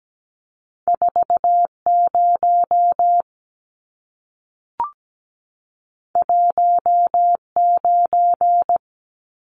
Morse code, heard as a keyed tone, about 17 wpm, 700 Hz. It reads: 40 19
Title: Numbers - 2-Digit 17wpm